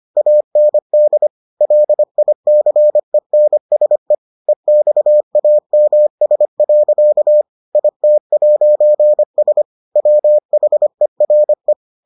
Morse code, heard as a keyed tone, about 25 wpm, 600 Hz. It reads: AND LICENSE EXAMS. IT'S WHERE